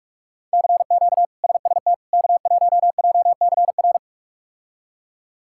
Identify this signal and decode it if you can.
Morse code, heard as a keyed tone, about 46 wpm, 700 Hz. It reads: CQ SST K1JCR
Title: K1USN SST QSO Practice - 45wpm